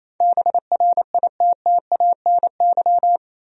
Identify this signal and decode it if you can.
Morse code, heard as a keyed tone, about 27 wpm, 700 Hz. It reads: BRITTANY
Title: USA Common Names - 28wpm